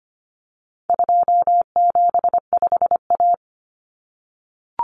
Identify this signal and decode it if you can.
Morse code, heard as a keyed tone, about 25 wpm, 700 Hz. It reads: 275A
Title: Common Cut Numbers - 4-Digit 25wpm